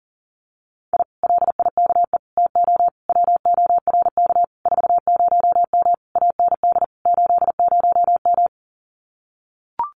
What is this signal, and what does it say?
Morse code, heard as a keyed tone, about 37 wpm, 700 Hz. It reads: I LIKE TO WORK 40M AND 80M
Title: Common QSO Phrases Encore 40wpm